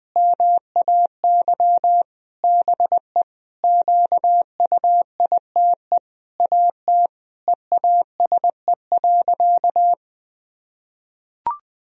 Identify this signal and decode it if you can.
Morse code, heard as a keyed tone, about 20 wpm, 700 Hz. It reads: MAY BE QUITE AT EASE.
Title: Sentences from Top 1000 Words - Review Part 1 - 20wpm